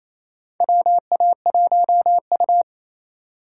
Morse code, heard as a keyed tone, about 27 wpm, 700 Hz. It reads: WA1U
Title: USA Call Signs - 4 Characters - 4x Character Spacing - 28wpm